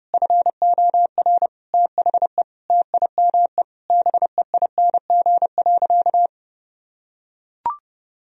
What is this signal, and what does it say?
Morse code, heard as a keyed tone, about 29 wpm, 700 Hz. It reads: FOR THE TIME BEING.
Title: Sentences from Top 1000 Words - Review Part 4 - 30wpm